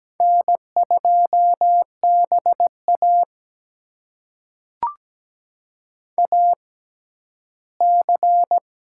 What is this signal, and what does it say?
Morse code, heard as a keyed tone, about 17 wpm, spaced as for 12 wpm, 700 Hz. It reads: N2BA AC
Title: USA Call Signs - 4 Characters - 6x Character Spacing - 17wpm